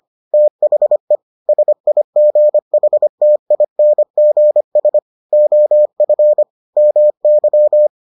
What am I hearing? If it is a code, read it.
Morse code, heard as a keyed tone, about 25 wpm, 600 Hz. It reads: THE SIGHTINGS OF MY